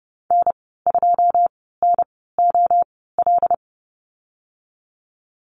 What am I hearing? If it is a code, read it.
Morse code, heard as a keyed tone, about 31 wpm, spaced as for 17 wpm, 700 Hz. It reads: N2NOL
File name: Lesson 14g - Number 2 - Call Signs 30wpm